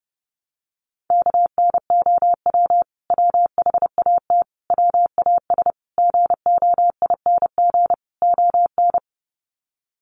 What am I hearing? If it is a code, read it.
Morse code, heard as a keyed tone, about 30 wpm, 700 Hz. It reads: KNOW WHAT WAS GOING ON
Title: Sets of 5 Words Encore2 30wpm